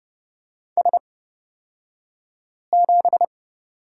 Morse code, heard as a keyed tone, about 30 wpm, spaced as for 5 wpm, 700 Hz. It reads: S7
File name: Lesson 37d - Letter x - Two Characters 30wpm